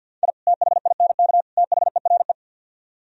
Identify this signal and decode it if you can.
Morse code, heard as a keyed tone, about 64 wpm, 700 Hz. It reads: I THINK THERE